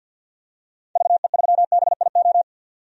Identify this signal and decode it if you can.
Morse code, heard as a keyed tone, about 51 wpm, 700 Hz. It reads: VE3BIO
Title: DXCC-100 Call Signs - 6 Characters - 50wpm